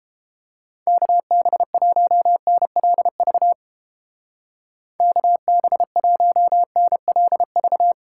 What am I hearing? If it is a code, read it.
Morse code, heard as a keyed tone, about 31 wpm, 700 Hz. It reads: KB1NLV KB1NLV